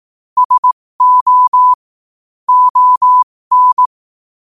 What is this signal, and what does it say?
Morse code, heard as a keyed tone, about 18 wpm, spaced as for 15 wpm, 1000 Hz. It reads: SO ON
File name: qotdfiles / 15wpm